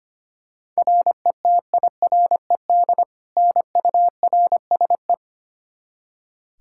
Morse code, heard as a keyed tone, about 25 wpm, 700 Hz. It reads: RETIRED NURSE
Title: Practice QSO - Exchange 3 - More Info - 25wpm